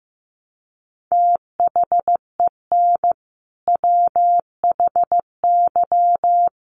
Morse code, heard as a keyed tone, about 15 wpm, 700 Hz. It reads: THEN WHY
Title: Sentences from Top 1200 Words 15wpm